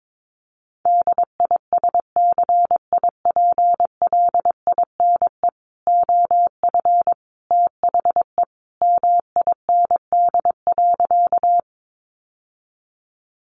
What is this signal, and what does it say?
Morse code, heard as a keyed tone, about 22 wpm, 700 Hz. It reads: DISCIPLINE OF THE MIND.
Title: Sentences from Top 900 Words 22wpm